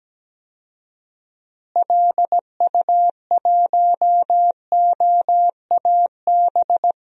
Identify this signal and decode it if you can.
Morse code, heard as a keyed tone, about 17 wpm, 700 Hz. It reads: LU1OAB
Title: DXCC-100 Call Signs - 6 Characters - Repeated 3x - 17wpm